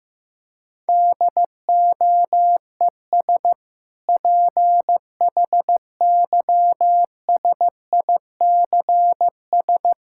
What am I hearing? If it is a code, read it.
Morse code, heard as a keyed tone, about 15 wpm, 700 Hz. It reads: DOES PHYSICS
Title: Sentences from Top 1000 Words - Review Part 5 - 15wpm